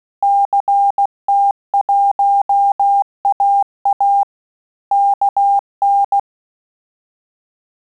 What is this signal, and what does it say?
Morse code, heard as a keyed tone, about 16 wpm, 800 Hz. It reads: CT1AA KN
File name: Telegraphy-group / QSO CW 15 PPM